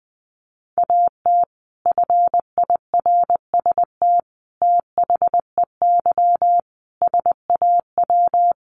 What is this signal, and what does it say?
Morse code, heard as a keyed tone, about 20 wpm, 700 Hz. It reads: AT FIRST THEY SAW
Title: Sentences from Top 600 Words 20wpm